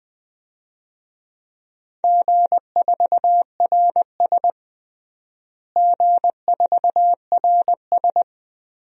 Morse code, heard as a keyed tone, about 19 wpm, 700 Hz. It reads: G4RS G4RS
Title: DXCC-100 Call Signs - 4 Characters - Repeated 2x - 20wpm